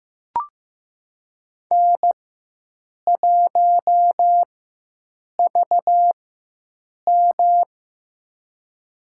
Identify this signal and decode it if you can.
Morse code, heard as a keyed tone, about 15 wpm, spaced as for 7 wpm, 700 Hz. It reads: N1VM
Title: USA Call Signs - 4x Character Spacing - 15wpm